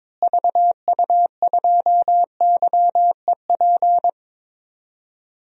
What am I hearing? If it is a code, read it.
Morse code, heard as a keyed tone, about 22 wpm, 700 Hz. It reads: VU2YEP
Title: DXCC-100 Call Signs - 6 Characters - Repeated 2x - 22wpm